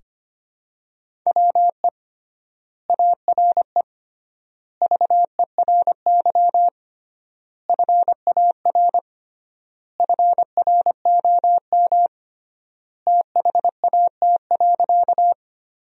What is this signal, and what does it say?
Morse code, heard as a keyed tone, about 25 wpm, 700 Hz. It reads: WE ARE VERY FAR FROM THAT.
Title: 3x Word-Spacing - Sentences from Top 300 Words - 25wpm